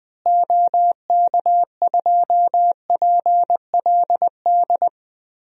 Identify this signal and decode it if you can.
Morse code, heard as a keyed tone, about 20 wpm, 700 Hz. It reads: OK2PLD
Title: DXCC-100 Call Signs - 6 Characters - Repeated 3x - 20wpm